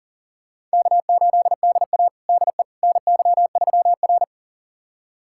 Morse code, heard as a keyed tone, about 40 wpm, 700 Hz. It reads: K8DA DE NY3R
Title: USA Callsign DE Callsign - 4 Characters - 40wpm